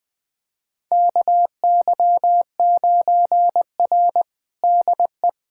Morse code, heard as a keyed tone, about 20 wpm, 700 Hz. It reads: KY9R DE
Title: USA Callsign DE Callsign - 4 Characters - 20wpm